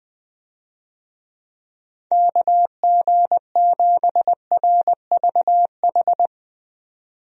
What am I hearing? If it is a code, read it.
Morse code, heard as a keyed tone, about 21 wpm, 700 Hz. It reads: KG7RVH
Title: USA Call Signs - 6 Characters - 20wpm